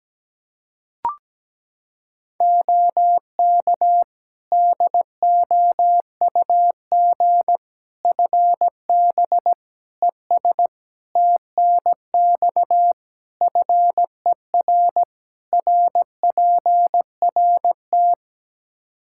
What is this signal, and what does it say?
Morse code, heard as a keyed tone, about 17 wpm, 700 Hz. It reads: OK DOUG FB ES TNX FER RPRT
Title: K1USN SST QSO Practice 17wpm